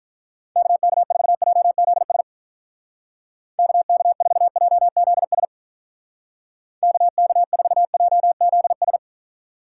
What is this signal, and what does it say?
Morse code, heard as a keyed tone, about 44 wpm, 700 Hz. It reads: KK4JZS KK4JZS KK4JZS